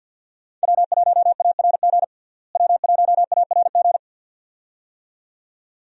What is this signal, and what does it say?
Morse code, heard as a keyed tone, about 48 wpm, 700 Hz. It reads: W1ARG W1ARG
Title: USA Call Signs - 5 Characters - Repeated 2x - 50wpm